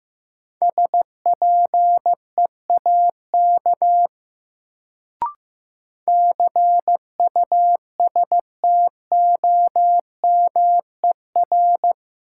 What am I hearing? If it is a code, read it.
Morse code, heard as a keyed tone, about 15 wpm, 700 Hz. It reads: SPEAK CUSTOMER
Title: Sentences from Top 1100 Words - New Word Drill Alt - 15wpm